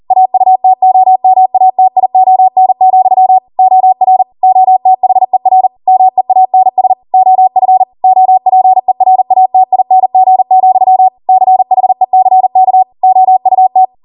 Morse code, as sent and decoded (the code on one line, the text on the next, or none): .- ..- - --- -- .- - .. --- -. --..-- / --- .-. / --- - .... .-. / -- . .- -. ... / --- ..-. / --- .--. .-. .- - .. -. --. --..-- / -.-. .... -.-. -.- / --- ..- -
AUTOMATION, OR OTHER MEANS OF OPERATING, CHECK OUT